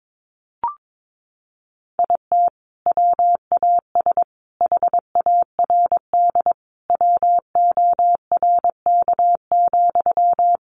Morse code, heard as a keyed tone, about 22 wpm, 700 Hz. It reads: IT WAS HARD WORK,
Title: Sentences from Top 600 Words 22wpm